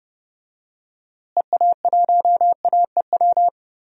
Morse code, heard as a keyed tone, about 30 wpm, 700 Hz. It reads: EA1AEW
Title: DXCC-100 Call Signs - 6 Characters - Repeated 3x - 30wpm